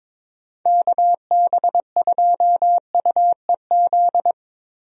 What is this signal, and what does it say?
Morse code, heard as a keyed tone, about 22 wpm, 700 Hz. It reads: KB2UEZ
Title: USA Call Signs - 6 Characters - 22wpm